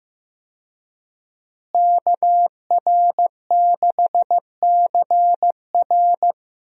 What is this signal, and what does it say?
Morse code, heard as a keyed tone, about 15 wpm, 700 Hz. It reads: KR6CR
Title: USA Call Signs - 5 Characters - Repeated 3x - 15wpm